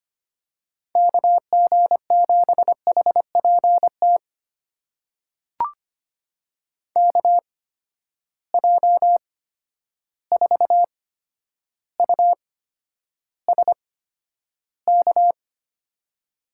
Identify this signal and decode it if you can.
Morse code, heard as a keyed tone, about 25 wpm, spaced as for 11 wpm, 700 Hz. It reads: KG7HPT KJ4USK
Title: USA Call Signs - 6 Characters - 8x Character Spacing - 25wpm